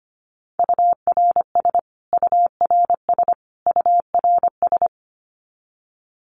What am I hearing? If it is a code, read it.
Morse code, heard as a keyed tone, about 25 wpm, 700 Hz. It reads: URS URS URS
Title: Abbreviations Repeated 3x - 25wpm